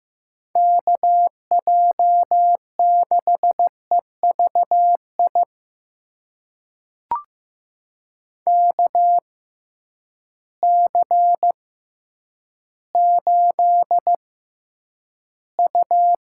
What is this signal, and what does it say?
Morse code, heard as a keyed tone, about 15 wpm, spaced as for 9 wpm, 700 Hz. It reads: KJ6EVI KC8U
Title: USA Call Signs - 6 Characters - 6x Character Spacing - 15wpm